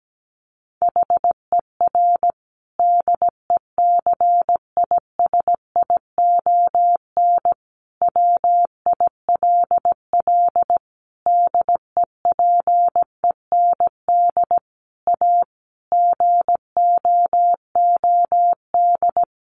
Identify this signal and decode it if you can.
Morse code, heard as a keyed tone, about 17 wpm, 700 Hz. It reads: HER DECISION WILL DEPEND A GOOD
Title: Sentences from Top 1300 Words 17wpm